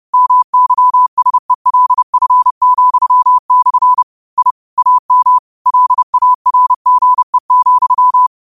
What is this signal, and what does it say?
Morse code, heard as a keyed tone, about 30 wpm, 1000 Hz. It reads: MYSELF,/ I AM LARGE,